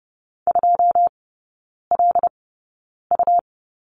Morse code, heard as a keyed tone, about 30 wpm, spaced as for 9 wpm, 700 Hz. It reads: 2LU
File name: Lesson 17e - Letter u - Three Characters 30wpm